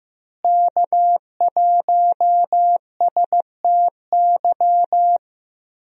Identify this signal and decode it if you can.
Morse code, heard as a keyed tone, about 15 wpm, 700 Hz. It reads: K1STY
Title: USA Call Signs - 5 Characters - Repeated 3x - 15wpm